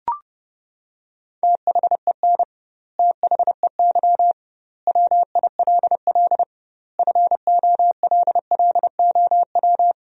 Morse code, heard as a keyed tone, about 32 wpm, 700 Hz. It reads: THEN THEY WILL FOLLOW
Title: 2x Word-Spacing - Sentences from Top 200 Words - 30wpm